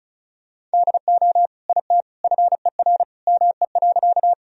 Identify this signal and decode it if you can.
Morse code, heard as a keyed tone, about 36 wpm, 700 Hz. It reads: DO IT FER ME.
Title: Sentences from Top 200 Words and Abbreviations 35wpm